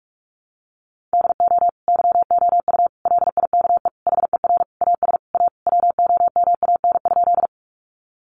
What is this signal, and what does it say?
Morse code, heard as a keyed tone, about 46 wpm, 700 Hz. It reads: DO YOU LIKE HER AS A WOMAN?
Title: Sentences from Top 1100 Words 45wpm